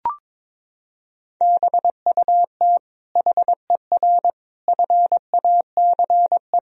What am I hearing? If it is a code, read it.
Morse code, heard as a keyed tone, about 22 wpm, 700 Hz. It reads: BUT HER FACE
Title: Sentences from Top 1000 Words - Review Part 2 - 22wpm